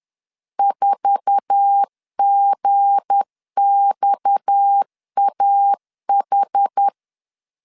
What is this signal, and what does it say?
Morse code, heard as a keyed tone, about 10 wpm, 800 Hz. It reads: 4GXAH
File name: CW / DET / 1 9 1500C S B 10wpm 4439